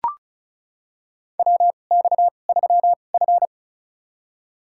Morse code, heard as a keyed tone, about 36 wpm, spaced as for 25 wpm, 700 Hz. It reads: WX3F
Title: USA Call Signs - 2x Character Spacing - 35wpm